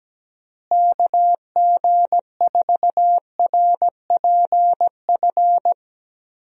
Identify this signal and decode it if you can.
Morse code, heard as a keyed tone, about 17 wpm, 700 Hz. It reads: KG4RPF
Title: USA Call Signs - 6 Characters - Repeated 3x - 17wpm